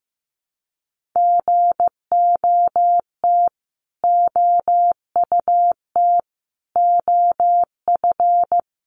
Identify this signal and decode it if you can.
Morse code, heard as a keyed tone, about 15 wpm, 700 Hz. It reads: GOT OUT OF
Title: Sets of 5 Words Encore 15wpm